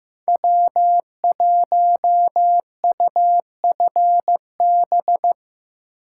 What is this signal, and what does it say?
Morse code, heard as a keyed tone, about 15 wpm, 700 Hz. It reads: W1UFB
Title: USA Call Signs - 5 Characters - 4x Character Spacing - 15wpm